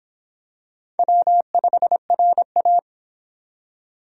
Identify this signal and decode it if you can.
Morse code, heard as a keyed tone, about 25 wpm, 700 Hz. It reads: W5RA